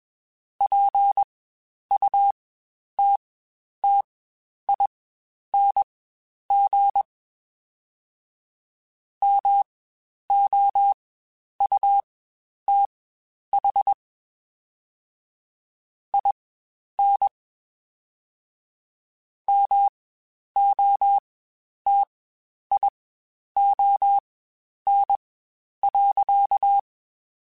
Morse code, heard as a keyed tone, about 21 wpm, spaced as for 10 wpm, 800 Hz. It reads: PUTTING MOUTH IN MOTION.
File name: Spaced / Chuckles Spaced-mp3s / BrainMouth sp @20